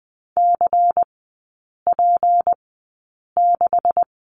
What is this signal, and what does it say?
Morse code, heard as a keyed tone, about 20 wpm, spaced as for 8 wpm, 700 Hz. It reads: CP6
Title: 152 Lesson 25e - Letter p - Three Characters 20wpm